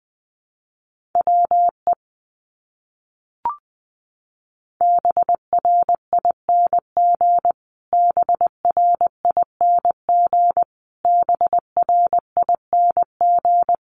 Morse code, heard as a keyed tone, about 20 wpm, 700 Hz. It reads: WE BRING BRING BRING
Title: Top 500 Words Repeated 20wpm